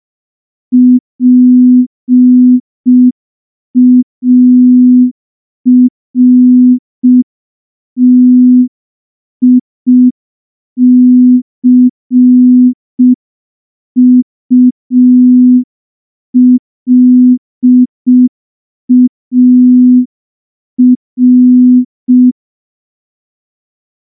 Morse code, heard as a keyed tone, about 5 wpm, 250 Hz. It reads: PARTICULAR
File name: morse / morse-training-05WPM